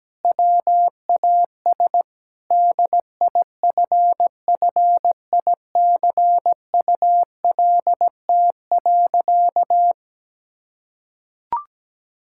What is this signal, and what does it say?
Morse code, heard as a keyed tone, about 17 wpm, 700 Hz. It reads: WAS DIFFICULT.
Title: Sentences from Top 1000 Words - Review Part 5 - 17wpm